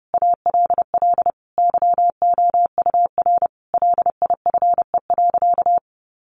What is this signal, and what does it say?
Morse code, heard as a keyed tone, about 28 wpm, 700 Hz. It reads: ALL YOUR LIFE.
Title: Sentences from Top 700 Words 30wpm